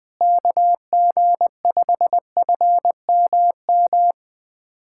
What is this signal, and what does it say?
Morse code, heard as a keyed tone, about 20 wpm, 700 Hz. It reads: KG5FMM
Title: USA Callsign DE Callsign - 6 Characters - 20wpm